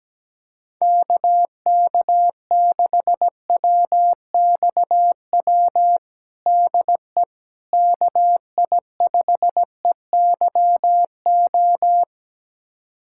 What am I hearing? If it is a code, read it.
Morse code, heard as a keyed tone, about 17 wpm, 700 Hz. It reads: KK6WXW DE KI5EYO
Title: USA Callsign DE Callsign - 6 Characters - 17wpm